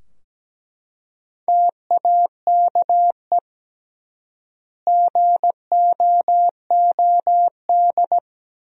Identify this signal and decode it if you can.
Morse code, heard as a keyed tone, about 17 wpm, 700 Hz. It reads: TAKE GOOD
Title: 3x Word-Spacing - Sentences from Top 500 Words - 17wpm